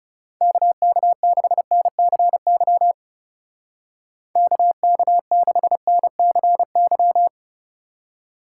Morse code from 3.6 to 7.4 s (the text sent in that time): KK6NCY